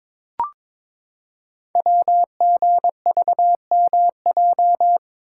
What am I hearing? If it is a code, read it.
Morse code, heard as a keyed tone, about 22 wpm, 700 Hz. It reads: WGVMJ
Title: Code Groups - 5 Letters - 22wpm